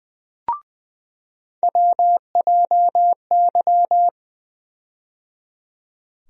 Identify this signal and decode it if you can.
Morse code, heard as a keyed tone, about 21 wpm, 700 Hz. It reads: WJY
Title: USA Call Sign Postfixes - 3 Letters 20wpm